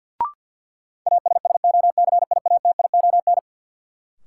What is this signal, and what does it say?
Morse code, heard as a keyed tone, about 50 wpm, 700 Hz. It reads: ASSOCIATION
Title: Sentences from Top 1100 Words - New Word Drill - 50wpm